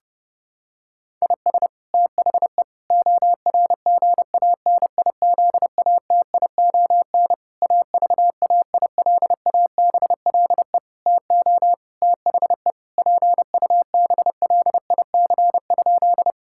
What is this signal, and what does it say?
Morse code, heard as a keyed tone, about 32 wpm, 700 Hz. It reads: IS THE ORGANIZATION AVAILABLE TO THE PUBLIC?
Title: Sentences from Top 1000 Words - Review Part 5 - 30wpm